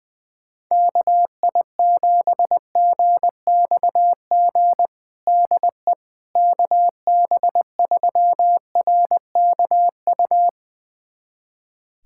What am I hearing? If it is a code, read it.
Morse code, heard as a keyed tone, about 20 wpm, 700 Hz. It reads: KI7GXG DE KB3RKU
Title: USA Callsign DE Callsign 20wpm